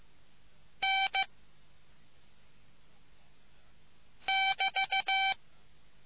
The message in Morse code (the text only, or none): N <BT>